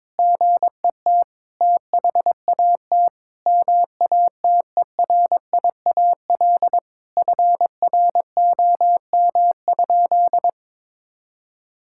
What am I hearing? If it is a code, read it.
Morse code, heard as a keyed tone, about 22 wpm, 700 Hz. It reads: GET THAT MATERIAL FROM?
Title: Sentences from Top 1000 Words - Review Part 5 - 22wpm